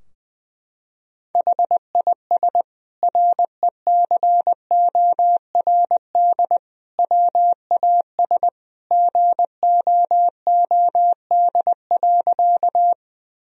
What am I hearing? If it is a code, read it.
Morse code, heard as a keyed tone, about 20 wpm, 700 Hz. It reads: HIS RECORD WAS GOOD.
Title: Sentences from Top 1000 Words - Review Part 3 - 20wpm